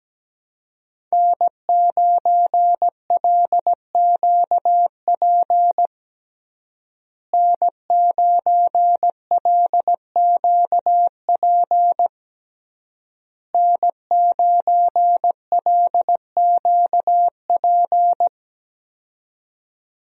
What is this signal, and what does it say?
Morse code, heard as a keyed tone, about 17 wpm, 700 Hz. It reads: N9LQP N9LQP N9LQP
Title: USA Call Signs - 5 Characters - Repeated 3x - 17wpm